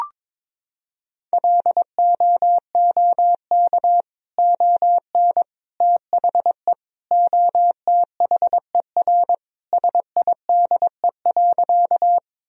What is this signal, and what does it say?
Morse code, heard as a keyed tone, about 22 wpm, 700 Hz. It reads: LOOK ON THE OTHER SIDE.
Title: Sentences from Top 1000 Words - Review Part 1 - 22wpm